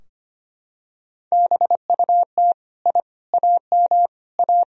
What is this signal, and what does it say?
Morse code, heard as a keyed tone, about 25 wpm, 700 Hz. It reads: BUT I AM A
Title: Sentences from Top 1000 Words - Review Part 3 - 25wpm